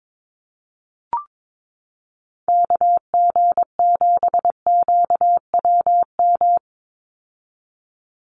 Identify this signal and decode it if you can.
Morse code, heard as a keyed tone, about 22 wpm, 700 Hz. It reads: KG7QWM
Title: USA Call Signs 22wpm